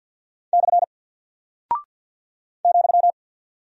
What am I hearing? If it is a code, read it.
Morse code, heard as a keyed tone, about 48 wpm, 700 Hz. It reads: / ,
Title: Single Character (Letter:Number:Punctuation:Prosign) Alt 50wpm